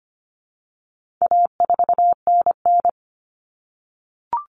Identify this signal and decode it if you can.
Morse code, heard as a keyed tone, about 25 wpm, 700 Hz. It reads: A4NN
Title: Common Cut Numbers - 4-Digit 25wpm